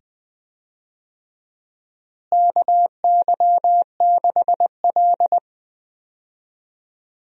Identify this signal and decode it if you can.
Morse code, heard as a keyed tone, about 20 wpm, 700 Hz. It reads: KY6L
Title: USA Call Signs - 4 Characters - Repeated 2x - 20wpm